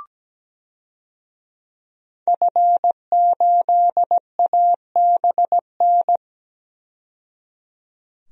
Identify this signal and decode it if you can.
Morse code, heard as a keyed tone, about 17 wpm, 700 Hz. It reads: F8ABN
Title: DXCC-100 Call Signs - 5 Characters - 17wpm